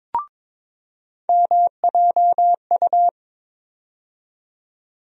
Morse code, heard as a keyed tone, about 22 wpm, 700 Hz. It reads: MJU
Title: USA Call Sign Postfixes - 3 Letters 22wpm